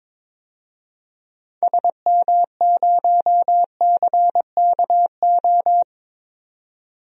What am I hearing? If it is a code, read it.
Morse code, heard as a keyed tone, about 21 wpm, 700 Hz. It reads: SM0CKO
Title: DXCC-100 Call Signs - 6 Characters - 22wpm